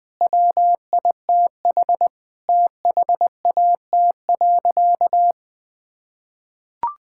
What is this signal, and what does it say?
Morse code, heard as a keyed tone, about 20 wpm, 700 Hz. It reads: WITH THAT.
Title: Sentences with 5-Word Phrases - 20wpm